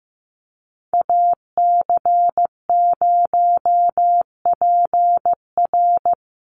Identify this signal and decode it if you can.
Morse code, heard as a keyed tone, about 15 wpm, 700 Hz. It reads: AC0PR
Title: NA SOTA Chaser Callsigns 15wpm